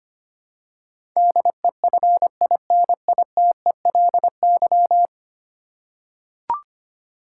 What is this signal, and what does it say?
Morse code, heard as a keyed tone, about 25 wpm, 700 Hz. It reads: DEFINITELY
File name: Ten Letter Words - 25wpm